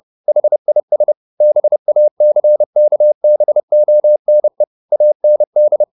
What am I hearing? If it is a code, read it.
Morse code, heard as a keyed tone, about 30 wpm, 600 Hz. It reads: HIS BACKBONE AND